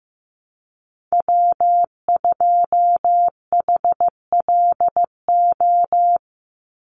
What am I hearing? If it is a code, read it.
Morse code, heard as a keyed tone, about 15 wpm, 700 Hz. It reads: W2HLO
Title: USA Call Signs 15wpm